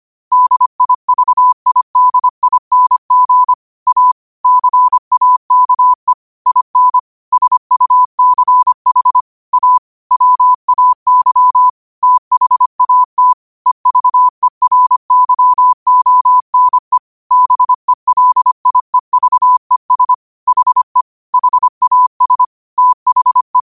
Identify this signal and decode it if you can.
Morse code, heard as a keyed tone, about 25 wpm, 1000 Hz. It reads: DIVIDING A CAKE IN SUCH A WAY THAT EVERYONE BELIEVES HE HAS THE
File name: qotdfiles / 25wpm